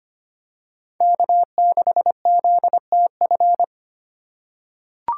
Morse code, heard as a keyed tone, about 25 wpm, 700 Hz. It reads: K6ZTF
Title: USA Call Signs - 5 Characters - 4x Character Spacing - 25wpm